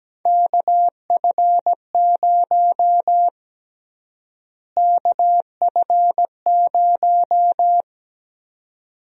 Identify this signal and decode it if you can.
Morse code, heard as a keyed tone, about 17 wpm, 700 Hz. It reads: KF0 KF0
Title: USA Call Sign Prefixes - Repeated 2x - 17wpm